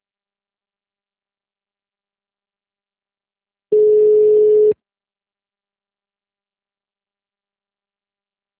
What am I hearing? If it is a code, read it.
Ringback tone (Europe, 425 Hz).